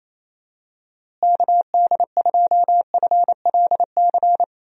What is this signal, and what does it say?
Morse code, heard as a keyed tone, about 28 wpm, 700 Hz. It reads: KD2FLC
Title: USA Call Signs - 6 Characters - Repeated 2x - 28wpm